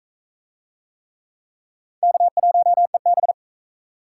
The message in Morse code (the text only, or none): K1EB